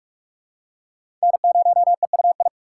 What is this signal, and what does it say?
Morse code, heard as a keyed tone, about 46 wpm, 700 Hz. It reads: N0EUI
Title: USA Call Signs - 3x Character Spacing - 45wpm